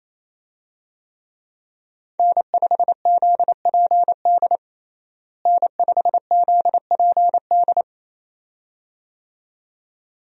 Morse code, heard as a keyed tone, about 28 wpm, 700 Hz. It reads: N5ZPD N5ZPD
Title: USA Call Signs - 5 Characters - Repeated 2x - 28wpm